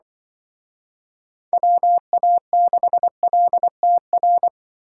Morse code, heard as a keyed tone, about 25 wpm, 700 Hz. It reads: WA6LTR
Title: Speed Racing - USA Call Signs 20wpm